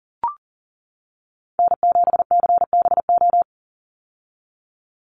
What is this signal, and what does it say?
Morse code, heard as a keyed tone, about 40 wpm, 700 Hz. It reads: N7CBO
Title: USA Call Signs 40wpm